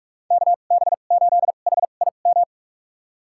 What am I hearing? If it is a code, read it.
Morse code, heard as a keyed tone, about 46 wpm, spaced as for 32 wpm, 700 Hz. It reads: KB8HIM
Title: USA Call Signs - 6 Characters - 2x Character Spacing - 45wpm